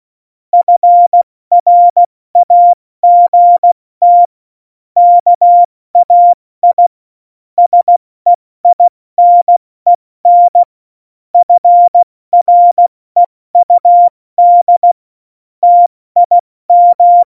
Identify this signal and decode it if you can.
Morse code, heard as a keyed tone, about 16 wpm, 700 Hz. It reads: FRAGT KAI SEINEN FREUD TIM
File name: Morsekurs / Witze / Witz 21. November 2017, 70 F80